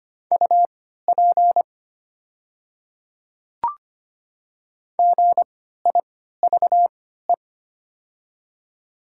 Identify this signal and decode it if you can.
Morse code, heard as a keyed tone, about 25 wpm, spaced as for 14 wpm, 700 Hz. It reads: UP GIVE
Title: 3x Character Spacing - Top 200 Words - 25wpm